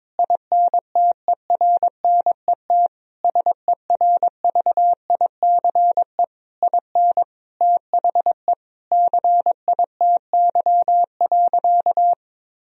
Morse code, heard as a keyed tone, about 22 wpm, 700 Hz. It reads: INTERNET SERVICE IN THE CITY.
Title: Sentences from Top 1000 Words - Review Part 4 - 22wpm